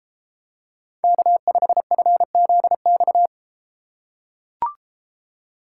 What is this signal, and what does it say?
Morse code, heard as a keyed tone, about 32 wpm, 700 Hz. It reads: K5FZX